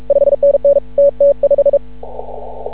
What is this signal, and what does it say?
Morse code, heard as a keyed tone, about 36 wpm, 560 Hz. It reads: 5NN TT5
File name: CW / ha6oi 2